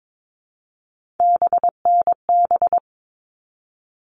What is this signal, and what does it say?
Morse code, heard as a keyed tone, about 22 wpm, 700 Hz. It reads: BNB
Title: All Cut Numbers - 3-Digit 22wpm